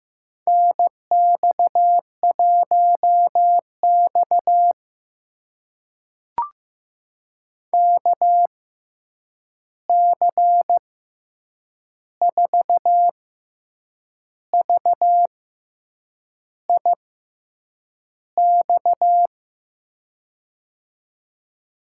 Morse code, heard as a keyed tone, about 15 wpm, spaced as for 7 wpm, 700 Hz. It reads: NX1X KC4VIX
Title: USA Call Signs - 6x Character Spacing - 15wpm